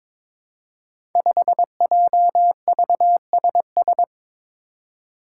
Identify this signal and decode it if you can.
Morse code, heard as a keyed tone, about 22 wpm, 700 Hz. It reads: 5JVSS
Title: Five Challenging Letter-Number Combinations 22wpm